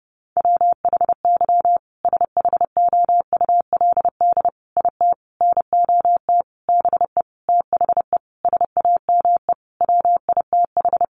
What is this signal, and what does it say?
Morse code, heard as a keyed tone, about 30 wpm, 700 Hz. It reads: WHY SHOULD IT NOT BE THE SAME WITH
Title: Sentences from Top 1400 Words 30wpm